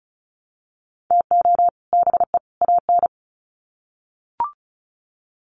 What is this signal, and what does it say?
Morse code, heard as a keyed tone, about 34 wpm, 700 Hz. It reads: TO BE AN
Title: Sets of 3 Words 35wpm